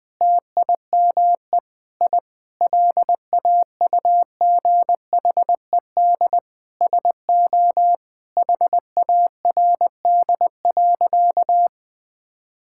Morse code, heard as a keyed tone, about 20 wpm, 700 Hz. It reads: TIME I LAUGHED SO HARD.
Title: Sentences with 7-Word Phrases - 20wpm